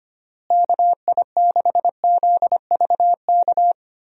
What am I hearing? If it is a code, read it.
Morse code, heard as a keyed tone, about 25 wpm, 700 Hz. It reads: KI6ZVK
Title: USA Call Signs - 6 Characters - Repeated 3x - 25wpm